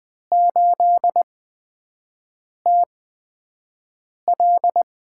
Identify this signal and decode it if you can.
Morse code, heard as a keyed tone, about 20 wpm, spaced as for 5 wpm, 700 Hz. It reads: 8TL
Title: USA Call Signs - 4 Characters - 8x Character Spacing - 20wpm